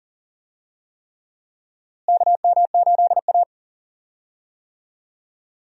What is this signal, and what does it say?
Morse code, heard as a keyed tone, about 40 wpm, 700 Hz. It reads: KM8A
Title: USA Call Signs - 4 Characters - Repeated 3x - 40wpm